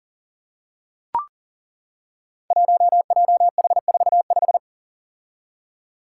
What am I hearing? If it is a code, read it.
Morse code, heard as a keyed tone, about 38 wpm, 700 Hz. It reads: 1JH45
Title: Five Challenging Letter-Number Combinations 40wpm